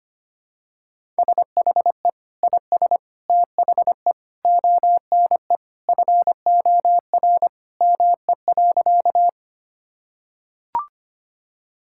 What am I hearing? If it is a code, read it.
Morse code, heard as a keyed tone, about 25 wpm, 700 Hz. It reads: SHE IS THE ONE FOR ME.
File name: practice / 25wpm / Sentences with 2-Word Phrases - 25wpm